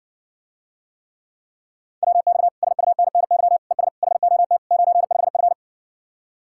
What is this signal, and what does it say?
Morse code, heard as a keyed tone, about 60 wpm, 700 Hz. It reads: WX SUNNY ES HOT 95F